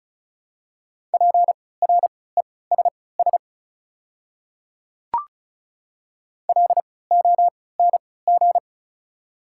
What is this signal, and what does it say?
Morse code, heard as a keyed tone, about 34 wpm, spaced as for 20 wpm, 700 Hz. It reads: PRESS LONG
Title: 3x Character Spacing - Top 300 Words - 35wpm